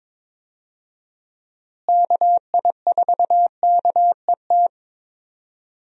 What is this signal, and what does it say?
Morse code, heard as a keyed tone, about 22 wpm, 700 Hz. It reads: KI4KET
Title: USA Call Signs - 6 Characters - Repeated 3x - 22wpm